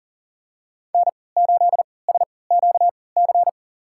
Morse code, heard as a keyed tone, about 40 wpm, spaced as for 23 wpm, 700 Hz. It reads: N8SQC
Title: USA Call Signs - 5 Characters - 3x Character Spacing - 40wpm